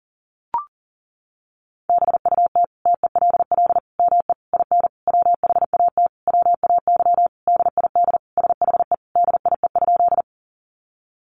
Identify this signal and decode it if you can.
Morse code, heard as a keyed tone, about 40 wpm, 700 Hz. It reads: BUT TELL ME IN WHAT WAY DID SHE DIE?
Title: Sentences from Top 1300 Words 40wpm